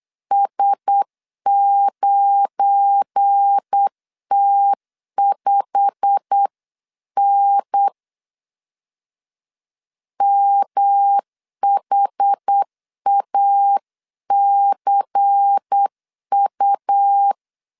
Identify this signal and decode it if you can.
Morse code, heard as a keyed tone, about 8 wpm, 800 Hz. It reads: S9T5N MHACU
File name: LEZIONI / 8 YJ45 / 8 YJ45 8wpm 3214